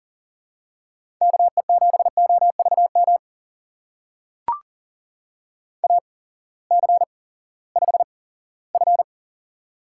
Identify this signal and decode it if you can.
Morse code, heard as a keyed tone, about 38 wpm, spaced as for 20 wpm, 690 Hz. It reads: KE7OVM AC5F
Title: USA Call Signs - 8x Character Spacing - 40wpm